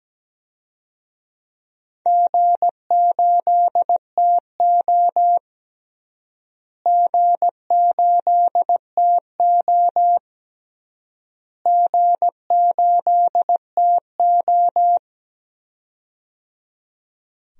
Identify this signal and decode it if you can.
Morse code, heard as a keyed tone, about 17 wpm, 700 Hz. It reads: G8TO G8TO G8TO
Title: DXCC-100 Call Signs - 4 Characters - Repeated 3x - 17wpm